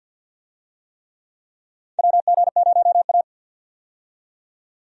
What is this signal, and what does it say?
Morse code, heard as a keyed tone, about 50 wpm, 700 Hz. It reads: WG0A
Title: USA Call Signs - 4 Characters - 50wpm